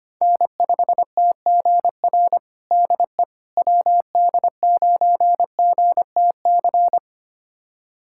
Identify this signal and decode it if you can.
Morse code, heard as a keyed tone, about 25 wpm, 700 Hz. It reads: N5TGR DE WD9GTC
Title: USA Callsign DE Callsign 25wpm